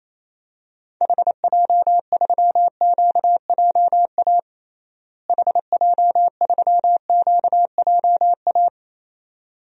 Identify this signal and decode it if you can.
Morse code, heard as a keyed tone, about 28 wpm, 700 Hz. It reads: HJ3QJA HJ3QJA
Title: DXCC-100 Call Signs - 6 Characters - Repeated 2x - 28wpm